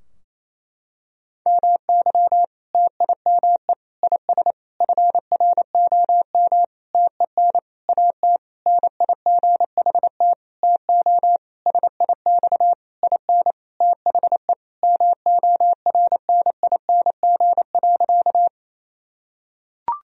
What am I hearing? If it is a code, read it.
Morse code, heard as a keyed tone, about 27 wpm, 700 Hz. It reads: MY TIME IS FROM TEN AT NIGHT TO SIX IN THE MORNING.
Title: Sentences from Top 1000 Words - Review Part 2 - 28wpm